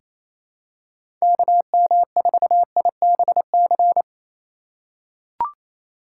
Morse code, heard as a keyed tone, about 28 wpm, 700 Hz. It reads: KM4IBC
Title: USA Call Signs - 6 Characters - 4x Character Spacing - 28wpm